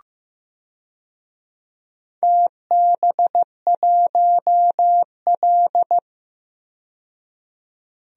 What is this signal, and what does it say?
Morse code, heard as a keyed tone, about 15 wpm, 700 Hz. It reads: TB1L
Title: DXCC-100 Call Signs - 4 Characters - 15wpm